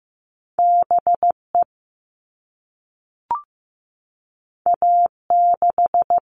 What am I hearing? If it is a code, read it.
Morse code, heard as a keyed tone, about 15 wpm, 700 Hz. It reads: BE A6
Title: All Cut Numbers - 2-Digit 15wpm